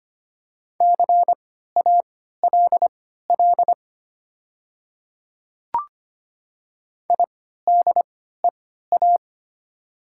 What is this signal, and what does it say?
Morse code, heard as a keyed tone, about 25 wpm, spaced as for 14 wpm, 700 Hz. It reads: CALL IDEA
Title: 3x Character Spacing - Top 300 Words - 25wpm